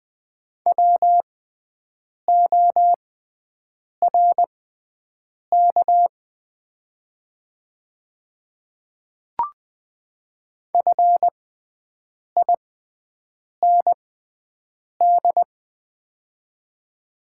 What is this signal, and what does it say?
Morse code, heard as a keyed tone, about 20 wpm, spaced as for 7 wpm, 700 Hz. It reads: WORK FIND
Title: 6x Character Spacing - Top 200 Words - 20wpm